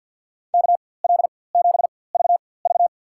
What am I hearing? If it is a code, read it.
Morse code, heard as a keyed tone, about 52 wpm, spaced as for 23 wpm, 700 Hz. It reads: KL7VV
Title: USA Call Signs - 5 Characters - 4x Character Spacing - 50wpm